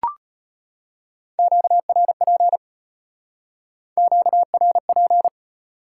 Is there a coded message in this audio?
Yes: Morse code from 1.4 to 5.3 s, heard as a keyed tone, about 33 wpm, 700 Hz. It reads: QRP QRP